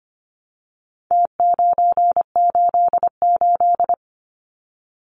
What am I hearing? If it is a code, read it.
Morse code, heard as a keyed tone, about 25 wpm, 700 Hz. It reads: T988
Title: Numbers - Contest Style - 4-Digit 25wpm